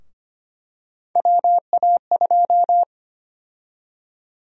Morse code, heard as a keyed tone, about 25 wpm, 700 Hz. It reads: WA2
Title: USA Call Sign Prefixes - Repeated 2x - 25wpm